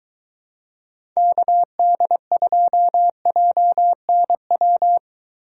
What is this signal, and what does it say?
Morse code, heard as a keyed tone, about 21 wpm, 700 Hz. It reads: KD2JNW